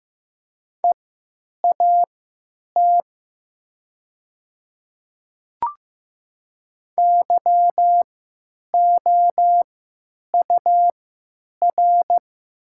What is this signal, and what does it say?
Morse code, heard as a keyed tone, about 15 wpm, spaced as for 9 wpm, 700 Hz. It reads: EAT YOUR
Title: 3x Character Spacing - Top 500 Words - 15wpm